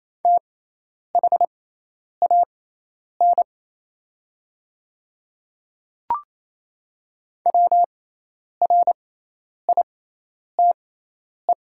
Morse code, heard as a keyed tone, about 28 wpm, spaced as for 10 wpm, 700 Hz. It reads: THAN WRITE
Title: 6x Character Spacing - Top 100 Words - 28wpm